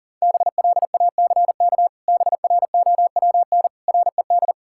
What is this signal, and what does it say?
Morse code, heard as a keyed tone, about 40 wpm, 700 Hz. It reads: BLACK BROWN RED
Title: Pangram Challenge v2 - 40wpm